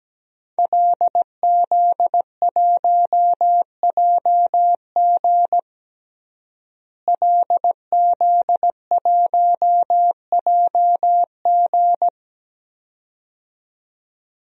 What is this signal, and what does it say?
Morse code, heard as a keyed tone, about 17 wpm, 700 Hz. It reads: LZ1JG LZ1JG
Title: DXCC-100 Call Signs - 5 Characters - Repeated 2x - 17wpm